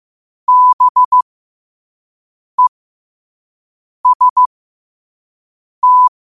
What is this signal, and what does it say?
Morse code, heard as a keyed tone, about 15 wpm, spaced as for 5 wpm, 1000 Hz. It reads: BEST
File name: 5wpm